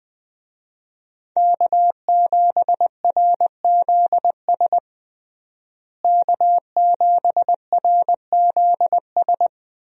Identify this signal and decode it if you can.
Morse code, heard as a keyed tone, about 20 wpm, 700 Hz. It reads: K7RZS K7RZS
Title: USA Call Signs - 5 Characters - Repeated 3x - 20wpm